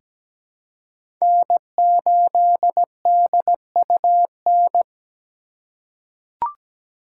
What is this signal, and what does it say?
Morse code, heard as a keyed tone, about 17 wpm, 700 Hz. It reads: N8DUN
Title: USA Call Signs - 5 Characters - 3x Character Spacing - 17wpm